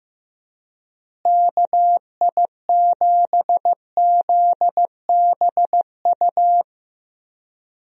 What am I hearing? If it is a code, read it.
Morse code, heard as a keyed tone, about 16 wpm, 700 Hz. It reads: KI7ZBU
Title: USA Call Signs - 6 Characters - Repeated 3x - 15wpm